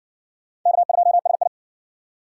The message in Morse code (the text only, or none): K2SI